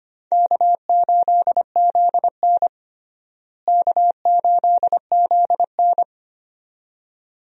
Morse code, heard as a keyed tone, about 25 wpm, 700 Hz. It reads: K8ZN K8ZN
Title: USA Call Signs - 4 Characters - Repeated 2x - 25wpm